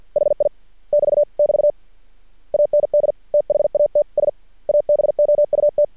Morse code, heard as a keyed tone, about 48 wpm, 590 Hz. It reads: HI <BT> <BT> AND THATS ABOUT
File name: archived / CRSnet